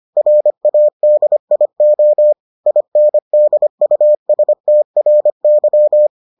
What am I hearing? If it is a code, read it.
Morse code, heard as a keyed tone, about 25 wpm, 600 Hz. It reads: RADIO INDUSTRY